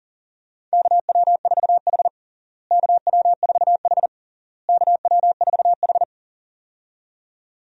Morse code, heard as a keyed tone, about 40 wpm, 700 Hz. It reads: KW4H KW4H KW4H
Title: USA Call Signs - 4 Characters - Repeated 3x - 40wpm